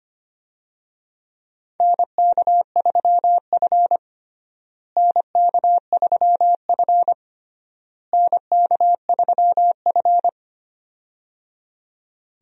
Morse code, heard as a keyed tone, about 25 wpm, 700 Hz. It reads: NK3F NK3F NK3F
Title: USA Call Signs - 4 Characters - Repeated 3x - 25wpm